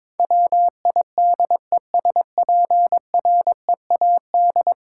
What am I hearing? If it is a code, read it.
Morse code, heard as a keyed tone, about 21 wpm, 700 Hz. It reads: WIDESPREAD
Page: morse-code-ninja-practice-22wpm